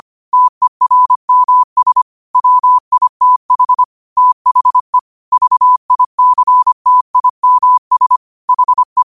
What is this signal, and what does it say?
Morse code, heard as a keyed tone, about 25 wpm, 1000 Hz. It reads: TERMS WITH THE VICTIMS HE